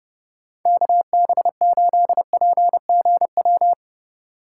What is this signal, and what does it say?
Morse code, heard as a keyed tone, about 30 wpm, 700 Hz. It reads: KB8PGW
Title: K1USN SST QSO Practice - 30wpm